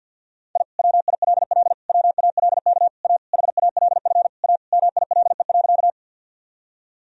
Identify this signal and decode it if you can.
Morse code, heard as a keyed tone, about 52 wpm, 700 Hz. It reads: I WILL WALK A HALF A MILE.
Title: Sentences from Top 1000 Words - Review Part 1 - 50wpm